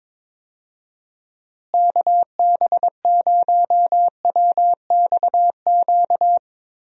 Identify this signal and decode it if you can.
Morse code, heard as a keyed tone, about 22 wpm, 700 Hz. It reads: KB0WXQ
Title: USA Call Signs - 6 Characters - Repeated 2x - 22wpm